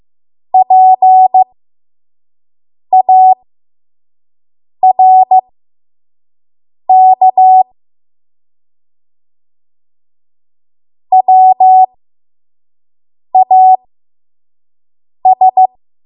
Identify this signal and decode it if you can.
Morse code, heard as a keyed tone, about 15 wpm, spaced as for 5 wpm, 750 Hz. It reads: PARK WAS
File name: Archive / 5 WPM / 05WPM